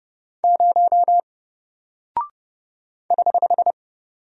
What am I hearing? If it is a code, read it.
Morse code, heard as a keyed tone, about 30 wpm, 700 Hz. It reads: 0 <HH>
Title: Single Character (Letter:Number:Punctuation:Prosign) Alt 30wpm